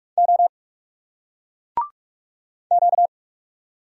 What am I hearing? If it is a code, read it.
Morse code, heard as a keyed tone, about 38 wpm, 700 Hz. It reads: O Q